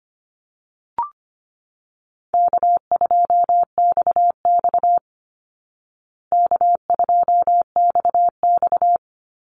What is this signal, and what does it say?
Morse code, heard as a keyed tone, about 25 wpm, 700 Hz. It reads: K2XX K2XX
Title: Easy Call Signs Repeated 3x - 25wpm